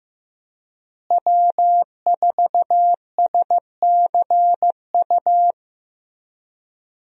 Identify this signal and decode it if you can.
Morse code, heard as a keyed tone, about 15 wpm, 700 Hz. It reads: W4SCU
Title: USA Call Signs - 5 Characters - Repeated 2x - 15wpm